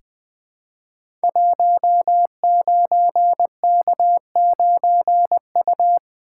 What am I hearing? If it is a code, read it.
Morse code, heard as a keyed tone, about 21 wpm, 700 Hz. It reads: 19K9U
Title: Code Groups - 5 Letter:Numbers - 20wpm